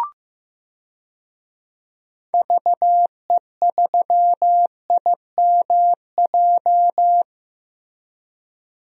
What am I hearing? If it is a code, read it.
Morse code, heard as a keyed tone, about 15 wpm, 700 Hz. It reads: VE3IMJ
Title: DXCC-100 Call Signs - 6 Characters - Repeated 2x - 15wpm